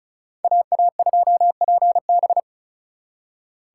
Morse code, heard as a keyed tone, about 45 wpm, 700 Hz. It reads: AA2PB